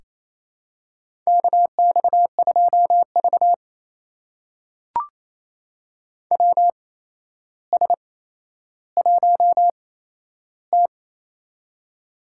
Morse code, heard as a keyed tone, about 28 wpm, spaced as for 12 wpm, 700 Hz. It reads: KX2V WS1T
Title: USA Call Signs - 4 Characters - 8x Character Spacing - 28wpm